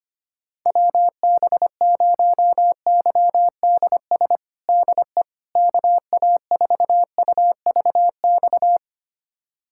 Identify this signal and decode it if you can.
Morse code, heard as a keyed tone, about 25 wpm, 700 Hz. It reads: WB0YDS DE KA4UVX
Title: USA Callsign DE Callsign - 6 Characters - 25wpm